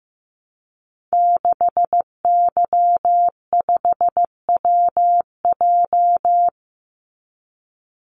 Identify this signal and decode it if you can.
Morse code, heard as a keyed tone, about 15 wpm, 700 Hz. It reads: 6Y5WJ
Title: Call Signs Repeated 3x - 15wpm